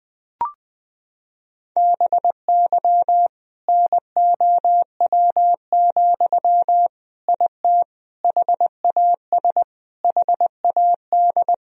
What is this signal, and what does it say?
Morse code, heard as a keyed tone, about 20 wpm, 700 Hz. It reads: BY NOW, IT HAS HAD
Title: Sentences from Top 1000 Words - Review Part 1 - 20wpm